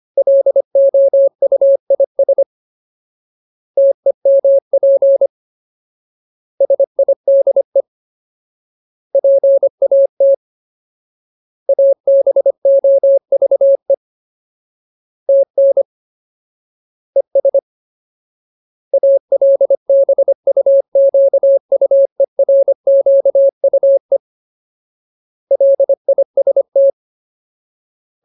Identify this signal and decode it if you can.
Morse code, heard as a keyed tone, about 25 wpm, 550 Hz. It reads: LOUIS TEMP SIDE PAT ABOVE TN ES ALBUQUERQUE LIST
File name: Ultimate Word List / 25WPM Ultimate